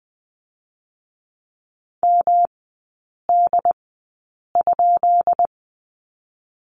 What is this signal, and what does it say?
Morse code, heard as a keyed tone, about 20 wpm, spaced as for 8 wpm, 700 Hz. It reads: MD?
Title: 138 Lesson 23e - Letter f - Three Characters 20wpm